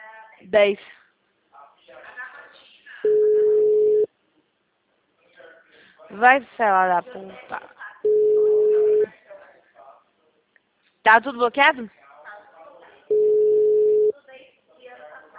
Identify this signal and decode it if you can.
Ringback tone (Europe, 425 Hz), with speech.